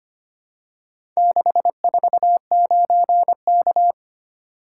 Morse code, heard as a keyed tone, about 25 wpm, 700 Hz. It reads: 649K
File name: practice / 25wpm / Four Letter-Number Combinations - 25wpm